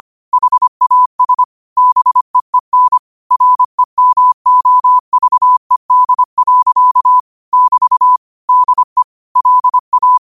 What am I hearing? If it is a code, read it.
Morse code, heard as a keyed tone, about 25 wpm, 1000 Hz. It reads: HAS BEEN REMOVED. <BT> DE LA